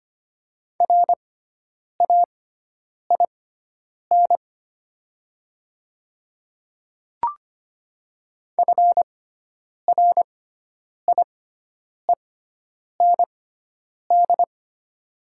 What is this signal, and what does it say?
Morse code, heard as a keyed tone, about 25 wpm, spaced as for 9 wpm, 700 Hz. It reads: RAIN FRIEND
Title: 6x Character Spacing - Top 300 Words - 25wpm